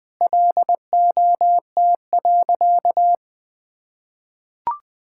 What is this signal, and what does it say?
Morse code, heard as a keyed tone, about 20 wpm, 700 Hz. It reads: LOT.
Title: Sentences with 3-Word Phrases - Spoken Beginning - 20wpm